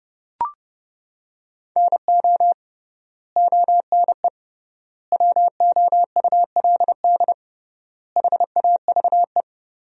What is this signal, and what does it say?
Morse code, heard as a keyed tone, about 30 wpm, 700 Hz. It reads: NO ONE WOULD HAVE
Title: 3x Word-Spacing - Sentences from Top 300 Words - 30wpm